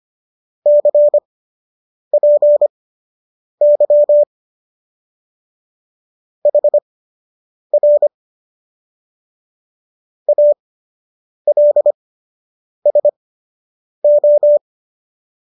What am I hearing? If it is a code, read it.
Morse code, heard as a keyed tone, about 25 wpm, spaced as for 8 wpm, 590 Hz. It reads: CPY HR ALSO